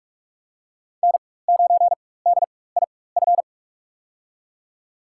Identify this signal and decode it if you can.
Morse code, heard as a keyed tone, about 48 wpm, spaced as for 21 wpm, 700 Hz. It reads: N9DIF
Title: USA Call Signs - 5 Characters - 4x Character Spacing - 45wpm